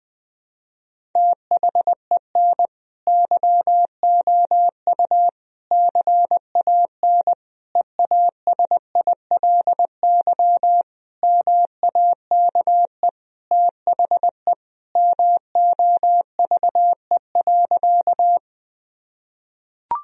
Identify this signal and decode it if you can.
Morse code, heard as a keyed tone, about 20 wpm, 700 Hz. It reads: THEN YOU CAN EASILY MAKE THE MOVE.
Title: Sentences from Top 1000 Words - Review Part 1 - 20wpm